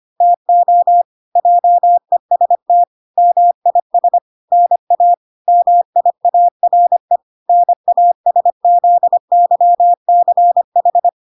Morse code, heard as a keyed tone, about 25 wpm, 700 Hz. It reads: TO JEST MIS NA MIARE NASZYCH